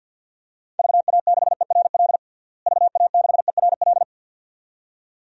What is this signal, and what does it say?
Morse code, heard as a keyed tone, about 53 wpm, 700 Hz. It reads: VA6ERL VA6ERL
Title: DXCC-100 Call Signs - 6 Characters - Repeated 2x - 50wpm